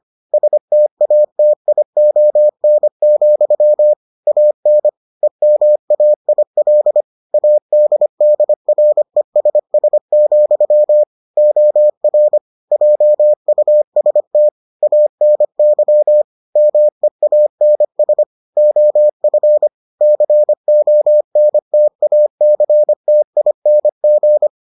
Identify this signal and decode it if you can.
Morse code, heard as a keyed tone, about 25 wpm, 600 Hz. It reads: STATION, AN EMAIL ADDRESS, OR JUST ANY MEANS OF CONTACTING